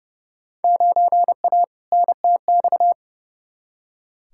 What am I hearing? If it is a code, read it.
Morse code, heard as a keyed tone, about 30 wpm, 700 Hz. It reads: 9A NTX
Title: Field Day Exchanges Encore - 30wpm